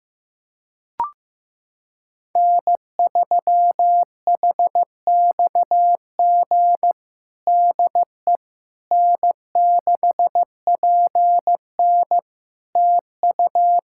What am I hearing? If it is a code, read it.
Morse code, heard as a keyed tone, about 15 wpm, 700 Hz. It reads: N3HXG DE N6PN TU
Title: K1USN SST QSO Practice 15wpm